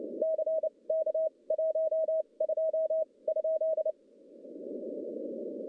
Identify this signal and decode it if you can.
Morse code, heard as a keyed tone, about 27 wpm, spaced as for 23 wpm, 640 Hz. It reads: CK 12 ?
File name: contests / CW-SS-k5tr- -1200